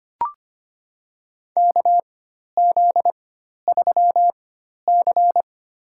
Morse code, heard as a keyed tone, about 25 wpm, spaced as for 12 wpm, 700 Hz. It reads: KZ3C
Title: USA Call Signs - 4x Character Spacing - 25wpm